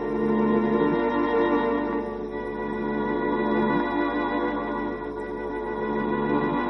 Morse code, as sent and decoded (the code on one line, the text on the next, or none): none